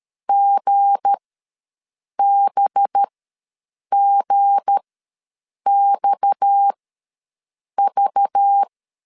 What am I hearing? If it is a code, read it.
Morse code, heard as a keyed tone, about 13 wpm, spaced as for 7 wpm, 800 Hz. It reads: GBGXV
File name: LEZIONI / 8 YJ45 / 8 Y ws 12wpm 3409